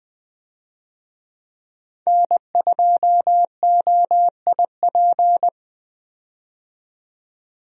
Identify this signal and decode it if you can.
Morse code, heard as a keyed tone, about 20 wpm, 700 Hz. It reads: N2OIP
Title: USA Call Signs - 5 Characters - 20wpm